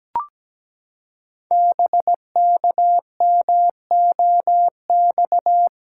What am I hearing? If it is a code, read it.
Morse code, heard as a keyed tone, about 17 wpm, 700 Hz. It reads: BKMOX
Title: Code Groups - 5 Letters - 17wpm